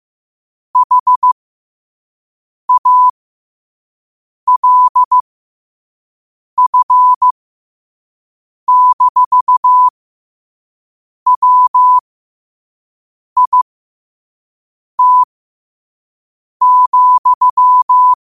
Morse code, heard as a keyed tone, about 15 wpm, spaced as for 5 wpm, 1000 Hz. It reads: HALF-WIT,
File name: qotdfiles / 5wpm